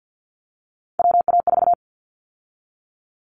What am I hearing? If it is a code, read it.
Morse code, heard as a keyed tone, about 50 wpm, 700 Hz. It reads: WA4
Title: Callsign Prefixes for Top 25 DXCC - 50wpm